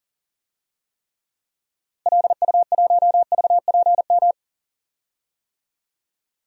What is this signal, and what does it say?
Morse code, heard as a keyed tone, about 37 wpm, 700 Hz. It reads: LU1VPM
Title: DXCC-100 Call Signs - 6 Characters - Repeated 2x - 40wpm